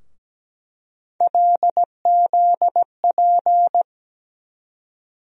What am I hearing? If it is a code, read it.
Morse code, heard as a keyed tone, about 17 wpm, 700 Hz. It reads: LZP
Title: USA Call Sign Postfixes - 3 Letters 17wpm